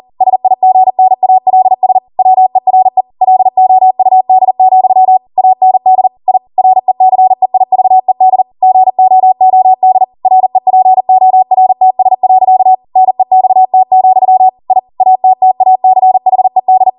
Morse code, heard as a keyed tone, about 40 wpm, 750 Hz. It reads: SIGNALS WERE LOUD, AND I RECEIVED GOOD REPORTS. NEXT, I ATTACHED